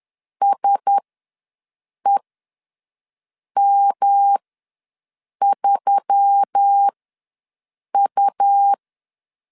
Morse code, heard as a keyed tone, about 10 wpm, spaced as for 6 wpm, 800 Hz. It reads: SEM3U
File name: LEZIONI / 5 LBVC / 5 L ws 10wpm 3636